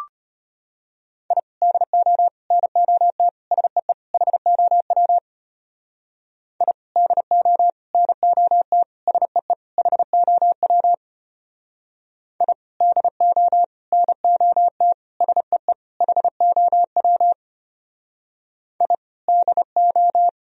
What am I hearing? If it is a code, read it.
Morse code, heard as a keyed tone, about 33 wpm, 700 Hz. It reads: I DO NOT SEE HOW I DO NOT SEE HOW I DO NOT SEE HOW I DO